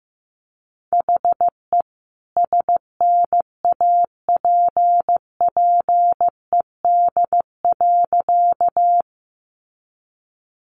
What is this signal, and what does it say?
Morse code, heard as a keyed tone, about 15 wpm, 700 Hz. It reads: HE SNAPPED.
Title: A Sound Of Thunder 15wpm